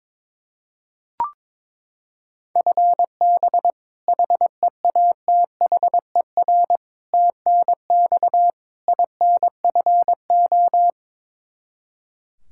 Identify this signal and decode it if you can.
Morse code, heard as a keyed tone, about 22 wpm, 700 Hz. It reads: FB HEATHER TNX INFO
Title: Practice QSO - Exchange 3 - More Info - 22wpm